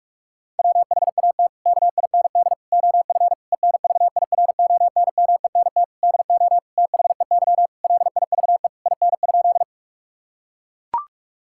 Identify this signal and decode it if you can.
Morse code, heard as a keyed tone, about 41 wpm, 700 Hz. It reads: WHAT KIND OF ENVIRONMENT DO THEY LIVE IN?
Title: Sentences from Top 1000 Words - Review Part 5 - 45wpm